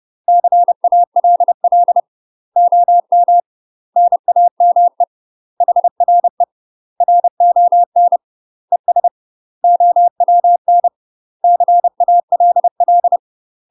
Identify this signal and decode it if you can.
Morse code, heard as a keyed tone, about 30 wpm, 700 Hz. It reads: CALL OM NAME HRE RON ES OWN CALL
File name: UK MASTER QSOs TRAINING / 30WPM / UK MASTER QSOs 0019